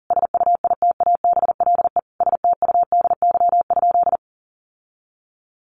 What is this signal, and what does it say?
Morse code, heard as a keyed tone, about 40 wpm, 700 Hz. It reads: SUITABLE STUDY?
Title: Sentences from Top 1400 Words 40wpm